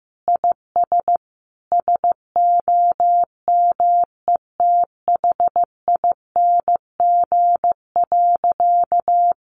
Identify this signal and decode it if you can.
Morse code, heard as a keyed tone, about 15 wpm, 700 Hz. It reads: IS SOMETHING.
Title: Sentences from Top 1200 Words 15wpm